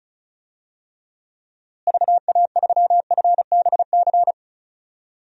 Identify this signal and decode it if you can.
Morse code, heard as a keyed tone, about 34 wpm, 700 Hz. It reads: VA3FBC
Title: DXCC-100 Call Signs - 6 Characters - Repeated 2x - 35wpm